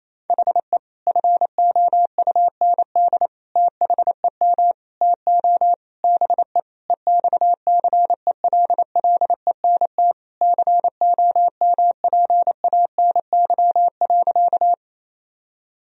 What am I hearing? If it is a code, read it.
Morse code, heard as a keyed tone, about 27 wpm, 700 Hz. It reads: HE FOUND THEM TO BE EXCELLENT COMPANY.
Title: Sentences from Top 1000 Words - Review Part 3 - 28wpm